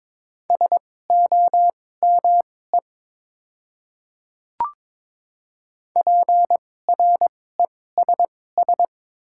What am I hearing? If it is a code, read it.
Morse code, heard as a keyed tone, about 21 wpm, spaced as for 16 wpm, 700 Hz. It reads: SOME PRESS
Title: 2x Character Spacing - Top 300 Words - 22wpm